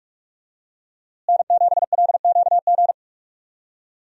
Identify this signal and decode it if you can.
Morse code, heard as a keyed tone, about 46 wpm, 700 Hz. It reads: N7LQG
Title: USA Call Signs - 5 Characters - Repeated 3x - 45wpm